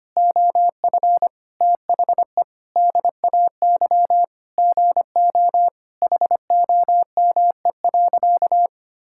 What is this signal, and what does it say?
Morse code, heard as a keyed tone, about 25 wpm, 700 Hz. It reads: OF THE DAY GO HOME.
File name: practice / 25wpm / Sentences with 3-Word Phrases - 25wpm